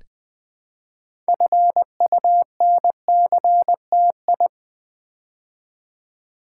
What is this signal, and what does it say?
Morse code, heard as a keyed tone, about 20 wpm, 700 Hz. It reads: FUNCTI
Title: Build A Word - 3 Letters At A Time - 8-Letter Words - 20wpm